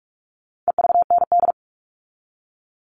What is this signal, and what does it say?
Morse code, heard as a keyed tone, about 44 wpm, 700 Hz. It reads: EVND
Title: All Cut Numbers - 4-Digit 45wpm